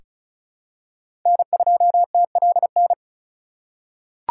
Morse code, heard as a keyed tone, about 36 wpm, 700 Hz. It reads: N2TLN
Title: USA Call Signs - 5 Characters - 2x Character Spacing - 35wpm